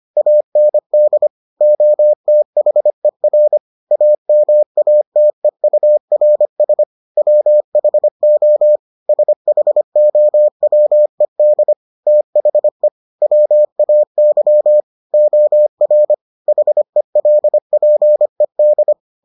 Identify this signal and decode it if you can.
Morse code, heard as a keyed tone, about 25 wpm, 600 Hz. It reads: AND OTHER AMATEURS WHO SHOWED THE WAY OR HELPED